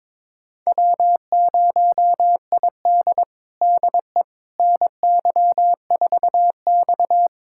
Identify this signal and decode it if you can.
Morse code, heard as a keyed tone, about 21 wpm, 700 Hz. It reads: W0ID DE NY4X
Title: USA Callsign DE Callsign - 4 Characters - 22wpm